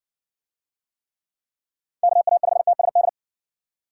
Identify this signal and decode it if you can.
Morse code, heard as a keyed tone, about 60 wpm, 700 Hz. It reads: KA5TSD